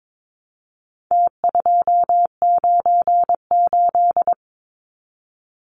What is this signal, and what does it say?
Morse code, heard as a keyed tone, about 22 wpm, 700 Hz. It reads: T298
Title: Numbers - Contest Style - 4-Digit 22wpm